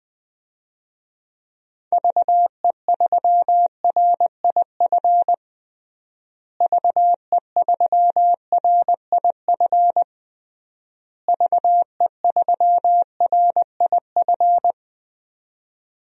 Morse code, heard as a keyed tone, about 20 wpm, 700 Hz. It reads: VE3RIF VE3RIF VE3RIF
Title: DXCC-100 Call Signs - 6 Characters - Repeated 3x - 20wpm